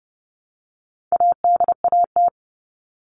Morse code, heard as a keyed tone, about 29 wpm, 700 Hz. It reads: ADAT
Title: All Cut Numbers - 4-Digit 30wpm